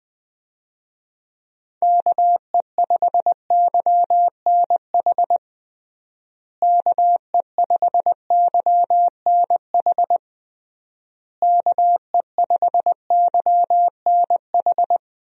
Morse code, heard as a keyed tone, about 20 wpm, 700 Hz. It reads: KE5YNH KE5YNH KE5YNH
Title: USA Call Signs - 6 Characters - Repeated 3x - 20wpm